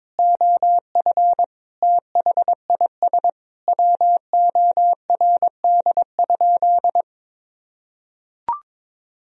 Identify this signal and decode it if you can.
Morse code, heard as a keyed tone, about 22 wpm, 700 Hz. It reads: OF THIS WORD?
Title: Sentences from Top 1000 Words - Review Part 4 - 22wpm